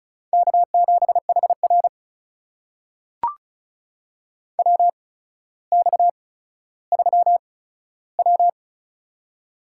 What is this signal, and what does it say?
Morse code, heard as a keyed tone, about 36 wpm, spaced as for 15 wpm, 700 Hz. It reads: K7HR WX3W
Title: USA Call Signs - 4 Characters - 8x Character Spacing - 35wpm